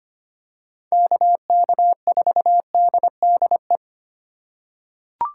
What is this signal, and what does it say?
Morse code, heard as a keyed tone, about 25 wpm, 700 Hz. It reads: KK4DDE
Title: USA Call Signs - 6 Characters - 3x Character Spacing - 25wpm